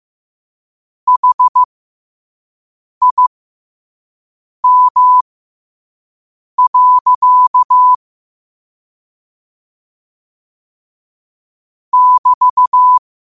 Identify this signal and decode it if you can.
Morse code, heard as a keyed tone, about 15 wpm, spaced as for 5 wpm, 1000 Hz. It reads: HIM. <BT>
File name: qotdfiles / 5wpm